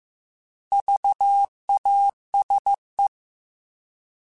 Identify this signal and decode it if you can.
Morse code, heard as a keyed tone, about 15 wpm, 800 Hz. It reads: VASE
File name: Alphabetical / Alphabetical mp3s / Vs @13